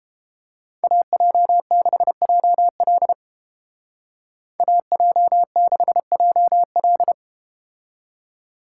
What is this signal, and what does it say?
Morse code, heard as a keyed tone, about 31 wpm, 700 Hz. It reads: AJ6JL AJ6JL